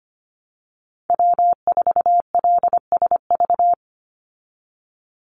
Morse code, heard as a keyed tone, about 25 wpm, 700 Hz. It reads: W4LSV
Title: Call Signs Repeated 2x - 25wpm